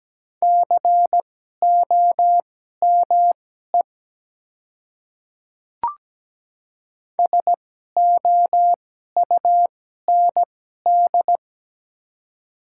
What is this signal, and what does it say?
Morse code, heard as a keyed tone, about 17 wpm, spaced as for 12 wpm, 700 Hz. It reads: COME SOUND
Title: 2x Character Spacing - Top 100 Words - 17wpm